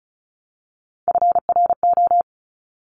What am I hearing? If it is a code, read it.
Morse code, heard as a keyed tone, about 36 wpm, 700 Hz. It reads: FRO
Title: Top Three Letter Combinations in Words 35wpm